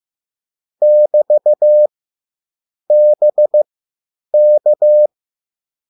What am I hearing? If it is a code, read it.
Morse code, heard as a keyed tone, about 15 wpm, spaced as for 9 wpm, 600 Hz. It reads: <BT> BK
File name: QSL / CW lesson